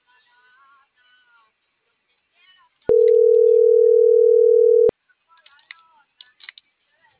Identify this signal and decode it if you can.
Ringback tone (North America), 440 + 480 Hz.